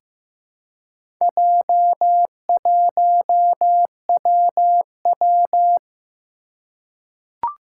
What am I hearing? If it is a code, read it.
Morse code, heard as a keyed tone, about 15 wpm, 700 Hz. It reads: J1WW
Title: Four Challenging Letter-Number Combinations 15wpm